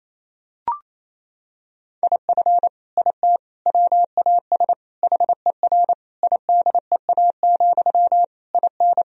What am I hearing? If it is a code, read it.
Morse code, heard as a keyed tone, about 28 wpm, 700 Hz. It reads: IF IT WAS HER IDEA, IN
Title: Sentences from Top 1000 Words - Review Part 2 - 28wpm